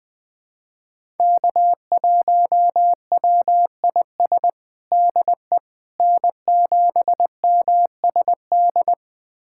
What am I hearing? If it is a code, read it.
Morse code, heard as a keyed tone, about 21 wpm, 700 Hz. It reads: K1WIS DE N7MSD
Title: USA Callsign DE Callsign 20wpm